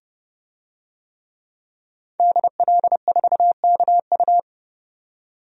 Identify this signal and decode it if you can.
Morse code, heard as a keyed tone, about 32 wpm, 700 Hz. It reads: DL4KU
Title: DXCC-100 Call Signs - 5 Characters - 30wpm